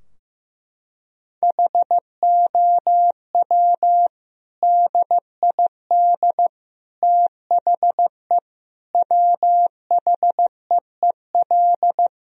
Morse code, heard as a keyed tone, about 15 wpm, 700 Hz. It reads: HOW DID THE WHEEL
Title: Sentences from Top 1000 Words - Review Part 3 - 15wpm